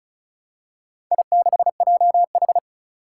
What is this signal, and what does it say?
Morse code, heard as a keyed tone, about 36 wpm, 700 Hz. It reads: I6JH
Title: Four Challenging Letter-Number Combinations 35wpm